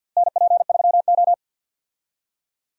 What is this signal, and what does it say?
Morse code, heard as a keyed tone, about 50 wpm, 700 Hz. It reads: NP3O
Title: USA Call Signs - 4 Characters - Repeated 2x - 50wpm